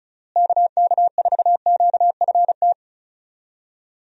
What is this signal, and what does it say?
Morse code, heard as a keyed tone, about 36 wpm, 700 Hz. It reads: KK4QFT
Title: USA Call Signs - 6 Characters - Repeated 3x - 35wpm